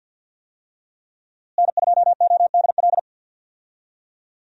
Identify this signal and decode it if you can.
Morse code, heard as a keyed tone, about 49 wpm, 700 Hz. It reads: N2ODL
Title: USA Call Signs - 5 Characters - Repeated 2x - 50wpm